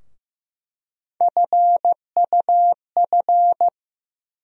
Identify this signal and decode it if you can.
Morse code, heard as a keyed tone, about 15 wpm, 700 Hz. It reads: FUF
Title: USA Call Sign Postfixes - 3 Letters 15wpm